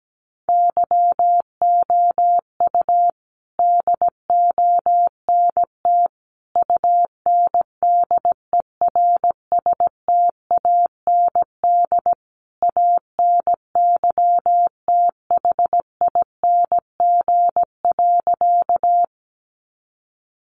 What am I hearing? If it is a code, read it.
Morse code, heard as a keyed tone, about 17 wpm, 700 Hz. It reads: YOU DONT UNDERSTAND ANYTHING.
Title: Sentences from Top 1000 Words 17wpm